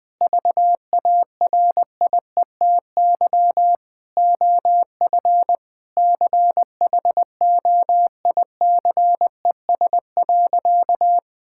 Morse code, heard as a keyed tone, about 20 wpm, 700 Hz. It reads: VARIETY OF CHOICES.
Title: Sentences with 3-Word Phrases - 20wpm